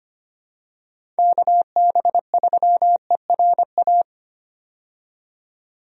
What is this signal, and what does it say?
Morse code, heard as a keyed tone, about 25 wpm, 700 Hz. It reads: KB3ERA
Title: USA Call Signs - 6 Characters - 25wpm